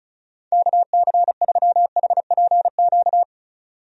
Morse code, heard as a keyed tone, about 36 wpm, 700 Hz. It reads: KC3HPQ
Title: USA Call Signs - 6 Characters - 35wpm